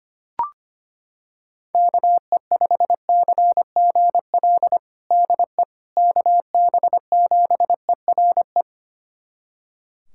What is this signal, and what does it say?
Morse code, heard as a keyed tone, about 25 wpm, 700 Hz. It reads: KE5CGL DE KB7ERE
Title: USA Callsign DE Callsign - 6 Characters - 25wpm